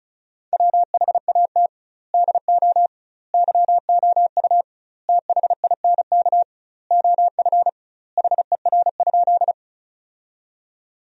Morse code, heard as a keyed tone, about 36 wpm, 700 Hz. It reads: WHAT DO YOU THINK OF HER?
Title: 2x Word-Spacing - Sentences from Top 200 Words - 35wpm